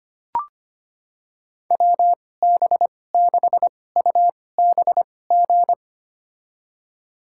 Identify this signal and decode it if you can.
Morse code, heard as a keyed tone, about 25 wpm, spaced as for 18 wpm, 700 Hz. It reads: WB6UBG
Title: USA Call Signs - 6 Characters - 2x Character Spacing - 25wpm